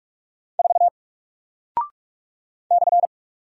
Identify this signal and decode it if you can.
Morse code, heard as a keyed tone, about 44 wpm, 700 Hz. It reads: 4 /
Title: Single Character (Letter:Number:Punctuation:Prosign) Alt 45wpm